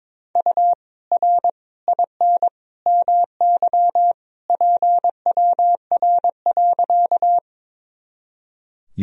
Morse code, heard as a keyed tone, about 22 wpm, 700 Hz. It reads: U R IN MY PWR.
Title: Sentences from Top 500 Words and Abbreviations 22wpm